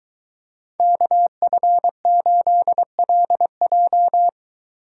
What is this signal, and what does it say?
Morse code, heard as a keyed tone, about 21 wpm, 700 Hz. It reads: KF8LJ